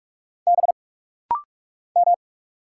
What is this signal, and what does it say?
Morse code, heard as a keyed tone, about 44 wpm, 690 Hz. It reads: B M
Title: Single Letter-Number - Rapid-Fire 45wpm